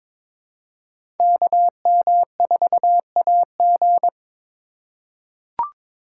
Morse code, heard as a keyed tone, about 21 wpm, 700 Hz. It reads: KM4AG
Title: USA Call Signs - 5 Characters - 3x Character Spacing - 22wpm